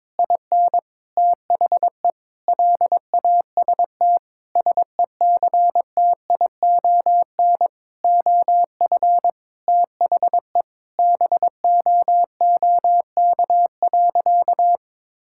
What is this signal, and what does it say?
Morse code, heard as a keyed tone, about 22 wpm, 700 Hz. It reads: IN THE LAST SECTION OF THE BOOK.
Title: Sentences from Top 1000 Words - Review Part 5 - 22wpm